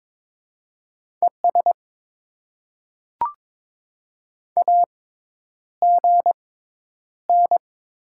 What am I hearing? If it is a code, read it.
Morse code, heard as a keyed tone, about 23 wpm, spaced as for 10 wpm, 700 Hz. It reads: ES AGN
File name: Top 20 QSO Elements - 6x Character Spacing - 22wpm